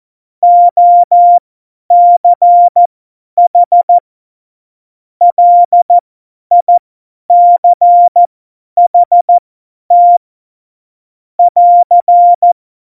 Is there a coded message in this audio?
Yes: Morse code from 0.4 to 12.5 s, heard as a keyed tone, about 14 wpm, spaced as for 10 wpm, 700 Hz. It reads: OCH LICHT <AR>